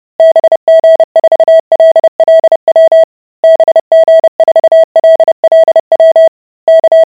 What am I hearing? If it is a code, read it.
Morse code, heard as a keyed tone, about 30 wpm, 650 Hz. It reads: BG4LLW BG4LLW K